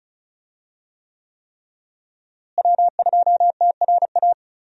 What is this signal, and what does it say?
Morse code, heard as a keyed tone, about 36 wpm, 700 Hz. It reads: W2TRA
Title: USA Call Signs - 5 Characters - Repeated 3x - 35wpm